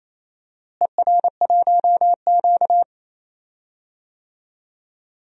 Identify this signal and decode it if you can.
Morse code, heard as a keyed tone, about 28 wpm, 700 Hz. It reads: ER1Q
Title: DXCC-100 Call Signs - 4 Characters - Repeated 2x - 28wpm